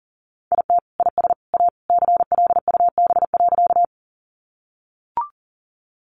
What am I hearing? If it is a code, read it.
Morse code, heard as a keyed tone, about 40 wpm, 700 Hz. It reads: IT IS A CLUB.
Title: Sentences from Top 1200 Words 40wpm